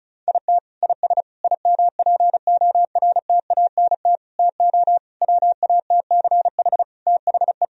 Morse code, heard as a keyed tone, about 36 wpm, 700 Hz. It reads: IT IS IMPORTANT TO WATCH THE
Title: Sentences from Top 1000 Words - Review Part 4 - 35wpm